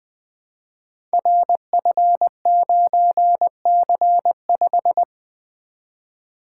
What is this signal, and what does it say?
Morse code, heard as a keyed tone, about 20 wpm, 700 Hz. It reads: RF9C5
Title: Code Groups - 5 Letter:Numbers - 20wpm